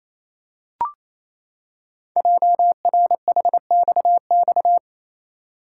Morse code, heard as a keyed tone, about 27 wpm, 700 Hz. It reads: JRHXX
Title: Code Groups - 5 Letters - 28wpm